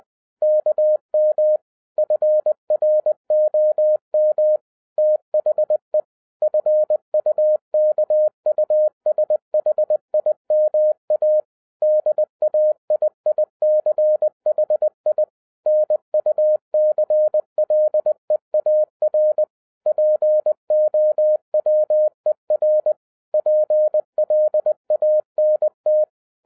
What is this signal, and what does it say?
Morse code, heard as a keyed tone, about 20 wpm, 610 Hz. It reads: KM FROM THE FUKUSHIMA DAIICHI NUCLEAR POWER PLANT
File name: 2011-10-mp3 / n03-TARO JR0QWW 20wpm